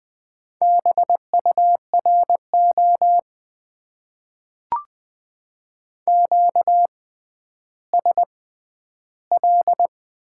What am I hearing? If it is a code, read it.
Morse code, heard as a keyed tone, about 20 wpm, spaced as for 11 wpm, 700 Hz. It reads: BURO QSL
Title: Top 90 QSO Elements - 6x Character Spacing - 20wpm